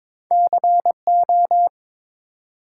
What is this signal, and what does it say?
Morse code, heard as a keyed tone, about 22 wpm, 700 Hz. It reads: CO
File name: ARRL Sections v2 - 22wpm